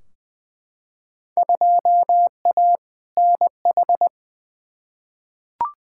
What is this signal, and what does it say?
Morse code, heard as a keyed tone, about 20 wpm, 700 Hz. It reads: 2A NH
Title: Field Day Exchanges Encore - 20wpm